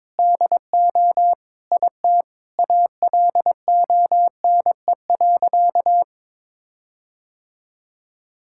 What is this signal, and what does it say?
Morse code, heard as a keyed tone, about 22 wpm, 700 Hz. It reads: DO IT ALONE.
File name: Sentences with 7-Word Phrases - 22wpm